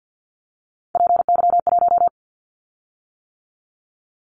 Morse code, heard as a keyed tone, about 50 wpm, 700 Hz. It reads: PY1
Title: Callsign Prefixes for Top 100 DXCC - 50wpm